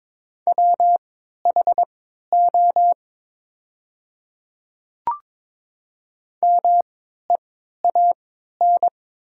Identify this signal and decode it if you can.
Morse code, heard as a keyed tone, about 22 wpm, spaced as for 13 wpm, 700 Hz. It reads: WHO MEAN
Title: 3x Character Spacing - Top 200 Words - 22wpm